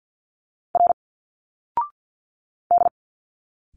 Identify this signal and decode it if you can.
Morse code, heard as a keyed tone, about 52 wpm, 700 Hz. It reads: R D
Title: Single Letters 50wpm